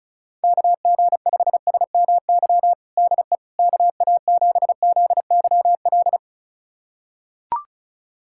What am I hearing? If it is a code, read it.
Morse code, heard as a keyed tone, about 36 wpm, 700 Hz. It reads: KG5SMY DE KA7ZYL
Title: USA Callsign DE Callsign - 6 Characters - 35wpm